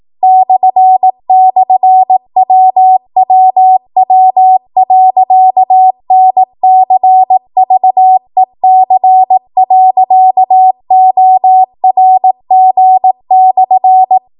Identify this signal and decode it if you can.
Morse code, heard as a keyed tone, about 18 wpm, 750 Hz. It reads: //WWW.NCVEC.ORG/